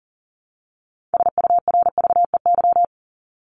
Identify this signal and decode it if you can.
Morse code, heard as a keyed tone, about 39 wpm, 700 Hz. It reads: SURVEY
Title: Top 1501-1750 Words 40wpm